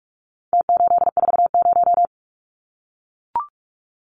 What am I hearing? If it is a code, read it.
Morse code, heard as a keyed tone, about 44 wpm, 700 Hz. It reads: T840
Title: Numbers - Contest Style - 4-Digit 45wpm